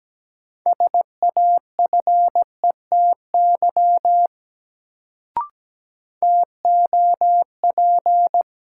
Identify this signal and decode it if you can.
Morse code, heard as a keyed tone, about 17 wpm, 700 Hz. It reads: SAFETY TOP
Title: Sentences from Top 700 Words - New Word Drill - 17wpm